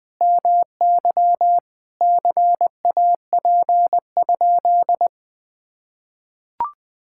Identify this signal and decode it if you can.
Morse code, heard as a keyed tone, about 20 wpm, 700 Hz. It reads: MY CAP?
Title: Sentences from Top 1600 Words 20wpm